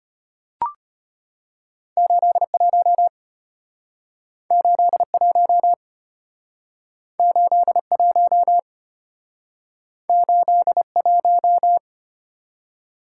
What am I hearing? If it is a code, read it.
Morse code, heard as a keyed tone, about 33 wpm, 700 Hz. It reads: 81 81 81 81